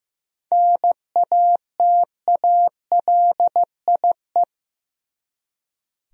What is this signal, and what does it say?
Morse code, heard as a keyed tone, about 15 wpm, 700 Hz. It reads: NATALIE
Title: USA Common Names - 15wpm